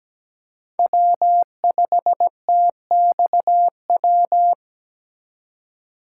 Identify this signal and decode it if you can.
Morse code, heard as a keyed tone, about 17 wpm, 700 Hz. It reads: W5TXW
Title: USA Call Signs - 3x Character Spacing - 17wpm